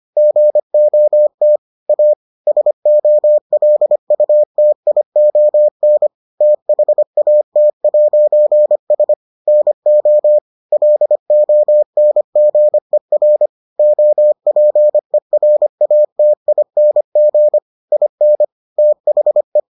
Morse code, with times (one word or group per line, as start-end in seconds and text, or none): 0.2-1.6 GOT
1.9-2.1 A
2.5-6.1 SOLUTION
6.4-9.1 THAT'S
9.5-10.4 NO
10.7-13.5 LONGER
13.8-17.6 OPERATING
17.9-18.4 IN
18.8-19.6 THE